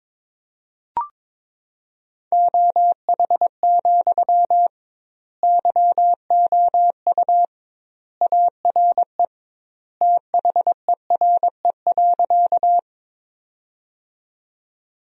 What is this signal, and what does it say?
Morse code, heard as a keyed tone, about 22 wpm, 700 Hz. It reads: OH, YOU ARE THERE.
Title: 2x Word-Spacing - Sentences from Top 500 Words - 22wpm